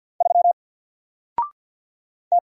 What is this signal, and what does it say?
Morse code, heard as a keyed tone, about 48 wpm, 700 Hz. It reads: 3 T
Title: Single Character (Letter:Number:Punctuation:Prosign) Alt 50wpm